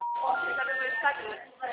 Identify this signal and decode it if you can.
Special information tone, with speech.